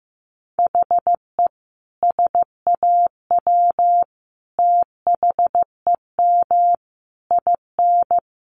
Morse code, heard as a keyed tone, about 14 wpm, 700 Hz. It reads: HE SAW THEM IN
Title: Sentences from Top 1300 Words 15wpm